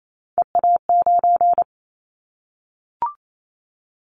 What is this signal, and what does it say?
Morse code, heard as a keyed tone, about 28 wpm, 700 Hz. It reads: EA9
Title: Callsign Prefixes for Top 100 DXCC - 28wpm